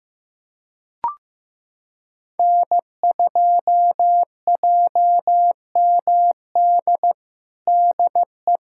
Morse code, heard as a keyed tone, about 15 wpm, 700 Hz. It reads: N2JMD DE
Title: USA Callsign DE Callsign 15wpm